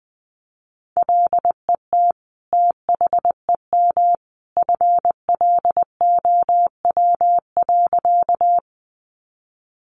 Morse code, heard as a keyed tone, about 20 wpm, 700 Hz. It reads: LET THEM FLOW.
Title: Sentences from Top 1200 Words 20wpm